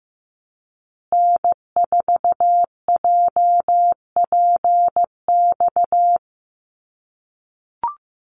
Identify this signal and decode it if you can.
Morse code, heard as a keyed tone, about 15 wpm, 700 Hz. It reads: N4JPX
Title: USA Call Signs Repeated 3x 15wpm